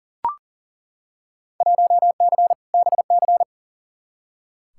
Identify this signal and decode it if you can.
Morse code, heard as a keyed tone, about 40 wpm, 700 Hz. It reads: 1C BC
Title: Field Day Exchanges Encore - 40wpm